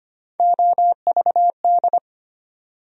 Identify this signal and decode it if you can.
Morse code, heard as a keyed tone, about 25 wpm, 700 Hz. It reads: OVD
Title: USA Call Sign Postfixes - 3 Letters 25wpm